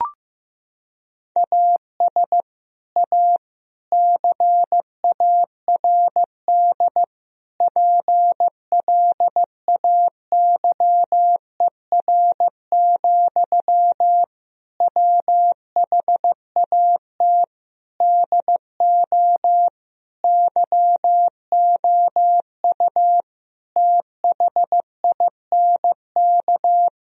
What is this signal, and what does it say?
Morse code, heard as a keyed tone, about 15 wpm, 700 Hz. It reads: AS A CARD PLAYER, WHAT DO YOU THINK
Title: Sentences from Top 1000 Words - Review Part 4 - 15wpm